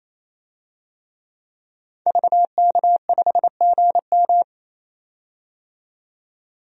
Morse code, heard as a keyed tone, about 28 wpm, 700 Hz. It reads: VK5GM
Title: DXCC-100 Call Signs - 5 Characters - 28wpm